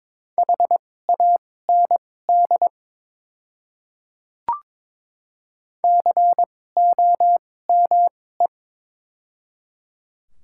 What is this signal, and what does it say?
Morse code, heard as a keyed tone, about 22 wpm, spaced as for 16 wpm, 690 Hz. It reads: HAND COME
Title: 2x Character Spacing - Top 200 Words - 22wpm